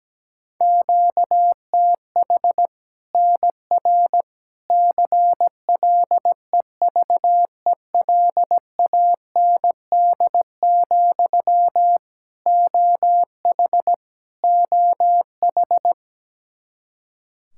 Morse code, heard as a keyed tone, about 17 wpm, 700 Hz. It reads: QTH NR CLEVELAND, OH OH
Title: K1USN SST QSO Practice 17wpm